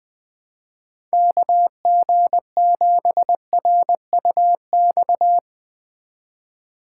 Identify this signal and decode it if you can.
Morse code, heard as a keyed tone, about 20 wpm, 700 Hz. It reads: KG7RUX
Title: USA Call Signs - 6 Characters - 3x Character Spacing - 20wpm